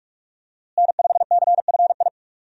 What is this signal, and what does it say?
Morse code, heard as a keyed tone, about 46 wpm, 700 Hz. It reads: N5CFI
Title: USA Call Signs - 5 Characters - Repeated 3x - 45wpm